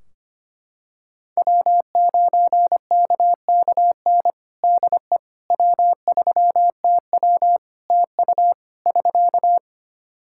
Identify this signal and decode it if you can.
Morse code, heard as a keyed tone, about 25 wpm, 700 Hz. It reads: W9KKN DE W3TW TU <SK>
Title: Practice QSO - Exchange 4 - Ending - 25wpm